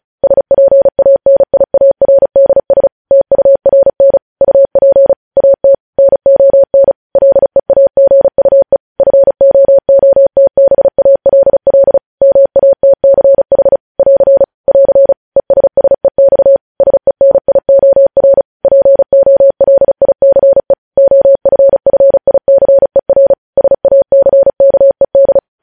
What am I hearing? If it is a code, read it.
Morse code, heard as a keyed tone, about 36 wpm, 550 Hz. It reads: SPANIARDS TURN UP AT NON LEAGUE FOOTBALL MATCH <AR> <AR> ESSEX SENIOR POLICE OFFICER SACKED
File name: podcast / News headlines-1x-35x35